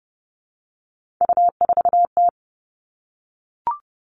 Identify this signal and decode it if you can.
Morse code, heard as a keyed tone, about 30 wpm, 700 Hz. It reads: U4T
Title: All Cut Numbers - 3-Digit 30wpm